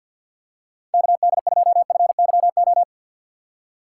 Morse code, heard as a keyed tone, about 48 wpm, 700 Hz. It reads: KD2FYO
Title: USA Call Signs - 6 Characters - 3x Character Spacing - 50wpm